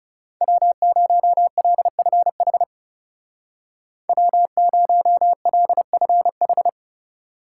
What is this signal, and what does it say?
Morse code, heard as a keyed tone, about 35 wpm, 700 Hz. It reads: W0LFH W0LFH
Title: Speed Racing - USA Call Signs 30wpm